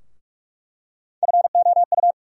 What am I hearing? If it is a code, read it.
Morse code, heard as a keyed tone, about 42 wpm, 700 Hz. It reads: FOU
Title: USA Call Sign Postfixes - 3 Letters 45wpm